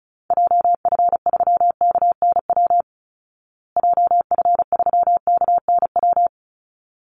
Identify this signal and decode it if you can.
Morse code, heard as a keyed tone, about 34 wpm, 700 Hz. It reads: JF3KNW JF3KNW
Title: Call Signs Repeated 3x - 35wpm